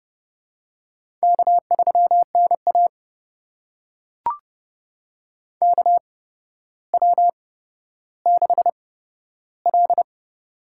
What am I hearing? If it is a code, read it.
Morse code, heard as a keyed tone, about 30 wpm, spaced as for 13 wpm, 700 Hz. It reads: K3NA KW6L
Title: USA Call Signs - 4 Characters - 8x Character Spacing - 30wpm